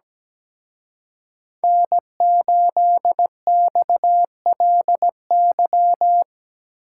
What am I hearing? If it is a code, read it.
Morse code, heard as a keyed tone, about 20 wpm, 700 Hz. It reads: N8XLY
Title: Speed Racing - USA 1x3 Call Signs 17wpm